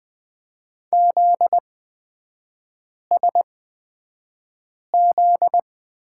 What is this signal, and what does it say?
Morse code, heard as a keyed tone, about 20 wpm, spaced as for 5 wpm, 700 Hz. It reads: ZSZ
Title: 243 Lesson 39e - Letter z - Three Characters 20wpm